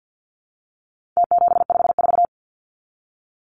Morse code, heard as a keyed tone, about 50 wpm, 700 Hz. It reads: T754
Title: Numbers - Contest Style - 4-Digit 50wpm